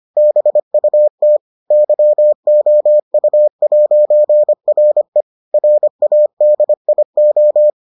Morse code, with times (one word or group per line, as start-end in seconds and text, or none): 0.2-1.4 BUT
1.7-5.2 YOU'RE
5.5-7.7 RADIO